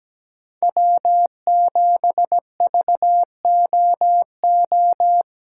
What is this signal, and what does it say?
Morse code, heard as a keyed tone, about 21 wpm, 700 Hz. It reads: W7VOO
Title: Speed Racing - USA 1x3 Call Signs 17wpm